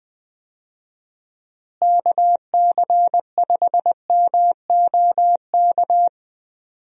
Morse code, heard as a keyed tone, about 20 wpm, 700 Hz. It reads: KC5MOK
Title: USA Call Signs - 6 Characters - Repeated 3x - 20wpm